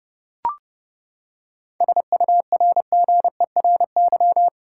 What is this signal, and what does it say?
Morse code, heard as a keyed tone, about 30 wpm, 700 Hz. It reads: SURGERY
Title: Seven Letter Words - 30wpm